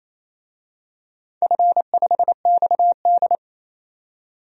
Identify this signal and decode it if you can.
Morse code, heard as a keyed tone, about 28 wpm, 700 Hz. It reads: F5XD
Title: DXCC-100 Call Signs - 4 Characters - 28wpm